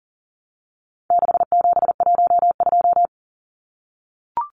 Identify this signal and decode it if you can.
Morse code, heard as a keyed tone, about 40 wpm, 700 Hz. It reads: 6712
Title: Numbers - 4-Digit 40wpm